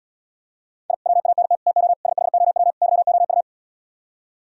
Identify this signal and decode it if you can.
Morse code, heard as a keyed tone, about 77 wpm, 700 Hz. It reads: I WANT TO SHOW YOU